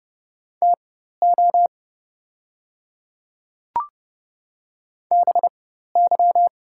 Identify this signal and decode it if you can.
Morse code, heard as a keyed tone, about 30 wpm, spaced as for 14 wpm, 700 Hz. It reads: TO BY
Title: 4x Character Spacing - Top 100 Words - 30wpm